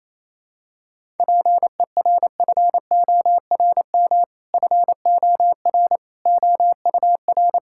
Morse code, heard as a keyed tone, about 28 wpm, 700 Hz. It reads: PERFORM FOR OUR
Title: Sentences from Top 1000 Words - Review Part 5 - 28wpm